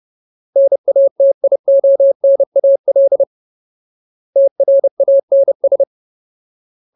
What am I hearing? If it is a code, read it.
Morse code, heard as a keyed tone, about 30 wpm, 550 Hz. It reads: NATIONAL TRANS
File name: Ultimate Word List / 30WPM Ultimate